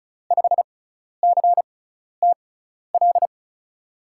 Morse code, heard as a keyed tone, about 36 wpm, spaced as for 12 wpm, 700 Hz. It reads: 5CTL
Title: USA Call Signs - 6x Character Spacing - 35wpm